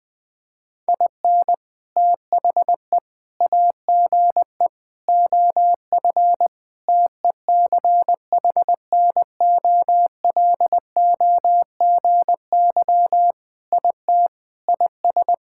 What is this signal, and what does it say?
Morse code, heard as a keyed tone, about 20 wpm, 700 Hz. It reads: IN THE AGE OF TECHNOLOGY IT IS
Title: Sentences with 3-Word Phrases - 20wpm